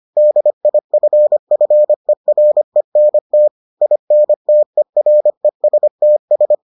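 Morse code, heard as a keyed tone, about 25 wpm, 600 Hz. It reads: DIFFERENT INTERESTS
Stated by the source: Morse edition of the Foundations of Amateur Radio episodes.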